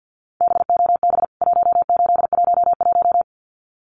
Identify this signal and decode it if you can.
Morse code, heard as a keyed tone, about 50 wpm, 700 Hz. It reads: BOB 1811
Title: CWT Exchange 50wpm